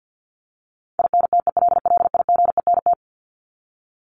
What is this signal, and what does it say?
Morse code, heard as a keyed tone, about 50 wpm, 700 Hz. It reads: INTELLIGENT
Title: Top 751-1000 Words 50wpm